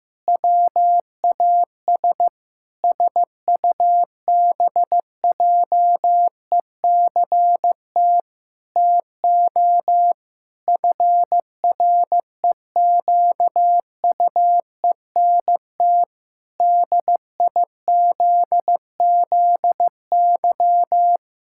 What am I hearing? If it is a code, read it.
Morse code, heard as a keyed tone, about 15 wpm, 700 Hz. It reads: WAS SUBJECT TO FREQUENT DIZZY
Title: Pangram Challenge v2 - 15wpm